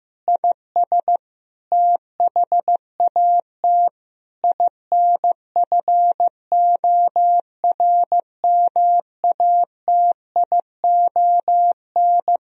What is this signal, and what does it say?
Morse code, heard as a keyed tone, about 15 wpm, 700 Hz. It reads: IS THAT INFORMATION
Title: Sentences from Top 1000 Words - Review Part 3 - 15wpm